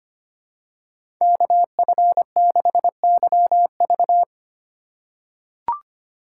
Morse code, heard as a keyed tone, about 25 wpm, 700 Hz. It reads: KF6YV
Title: USA Call Signs - 5 Characters - 2x Character Spacing - 25wpm